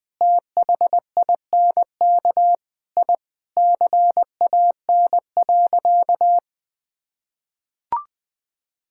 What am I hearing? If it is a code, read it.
Morse code, heard as a keyed tone, about 21 wpm, 700 Hz. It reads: THINK I CAN.
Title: Sentences with 4-Word Phrases - 20wpm